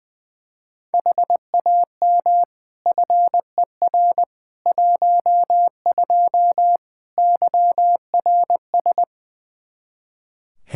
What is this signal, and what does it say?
Morse code, heard as a keyed tone, about 21 wpm, 700 Hz. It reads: HAM FER 12 YRS
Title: Practice QSO - Exchange 3 - More Info - 20wpm